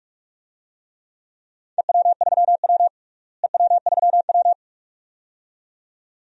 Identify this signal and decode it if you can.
Morse code, heard as a keyed tone, about 46 wpm, 700 Hz. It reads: EW3W EW3W
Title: DXCC-100 Call Signs - 4 Characters - Repeated 2x - 45wpm